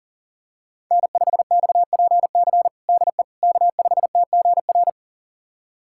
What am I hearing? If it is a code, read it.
Morse code, heard as a keyed tone, about 37 wpm, 700 Hz. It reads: N5XPC DE K5TGR
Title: USA Callsign DE Callsign - 5 Characters - 40wpm